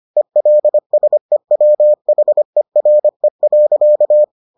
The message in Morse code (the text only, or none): ELSEWHERE.